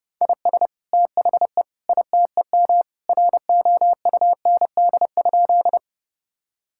Morse code, heard as a keyed tone, about 32 wpm, 700 Hz. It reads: IS THE ITEM ROUND?
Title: Sentences from Top 1000 Words - Review Part 6 - 30wpm